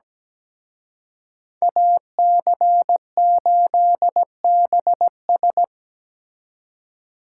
Morse code, heard as a keyed tone, about 20 wpm, 700 Hz. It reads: AC8BS
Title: Speed Racing - USA Call Signs 17wpm